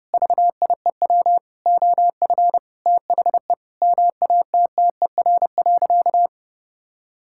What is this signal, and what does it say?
Morse code, heard as a keyed tone, about 32 wpm, 700 Hz. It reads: VIEW OF THE MATTER.
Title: Sentences from Top 1600 Words 30wpm